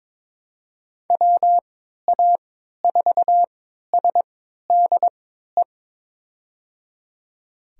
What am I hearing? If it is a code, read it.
Morse code, heard as a keyed tone, about 22 wpm, spaced as for 13 wpm, 700 Hz. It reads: WA4SDE
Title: USA Call Signs - 6 Characters - 3x Character Spacing - 22wpm